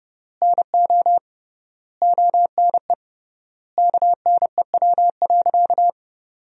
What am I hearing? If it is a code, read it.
Morse code, heard as a keyed tone, about 32 wpm, 700 Hz. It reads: NO ONE KNEW.
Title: 3x Word-Spacing - Sentences from Top 400 Words - 30wpm